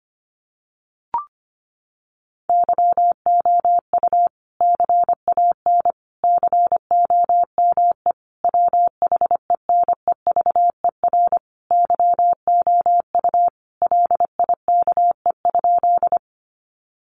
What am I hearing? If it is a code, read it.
Morse code, heard as a keyed tone, about 25 wpm, 700 Hz. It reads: YOU CAN COME WHENEVER YOU LIKE?
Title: Sentences from Top 1200 Words 25wpm